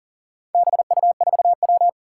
Morse code, heard as a keyed tone, about 40 wpm, 700 Hz. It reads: BU4W
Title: Four Challenging Letter-Number Combinations 40wpm